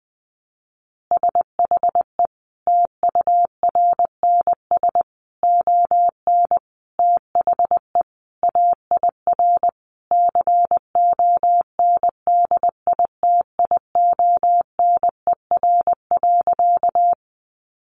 Morse code, heard as a keyed tone, about 20 wpm, 700 Hz. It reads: SHE TURNS ON THE AIR CONDITIONER.
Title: Additional Easy Phrases 20wpm